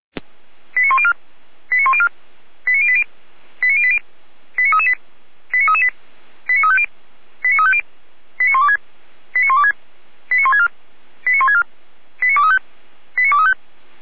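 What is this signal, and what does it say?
Five-tone selective call (ZVEI), heard as quick runs of five tones, tones of about 70 ms. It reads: 89104 89104 89080 89080 89338 89338 89350 89350 89126 89126 89164 89164 89235 89235
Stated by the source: ZVEI five-tone selective call 89025, received by SDR.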